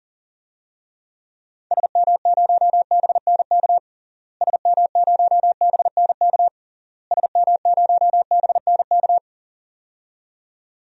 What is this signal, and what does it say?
Morse code, heard as a keyed tone, about 40 wpm, 700 Hz. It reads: SM0BNK SM0BNK SM0BNK
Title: DXCC-100 Call Signs - 6 Characters - Repeated 3x - 40wpm